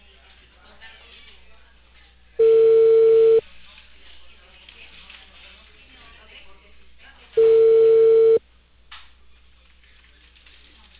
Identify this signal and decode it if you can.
Ringback tone (China, 450 Hz), with speech.